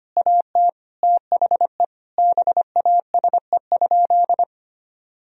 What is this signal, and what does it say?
Morse code, heard as a keyed tone, about 25 wpm, 700 Hz. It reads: AT THE BASE?
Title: Sentences from Top 1000 Words - Review Part 1 - 25wpm